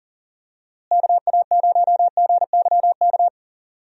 Morse code, heard as a keyed tone, about 38 wpm, 700 Hz. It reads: KA0GYK
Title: USA Call Signs - 6 Characters - Repeated 2x - 40wpm